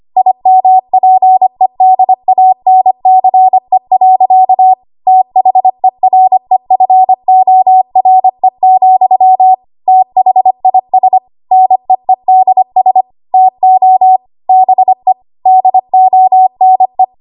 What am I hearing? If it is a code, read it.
Morse code, heard as a keyed tone, about 25 wpm, 750 Hz. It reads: IMPEDANCE. THEREFORE, THIS NEEDS TO BE DONE
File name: Archive / 25 WPM / 25WPM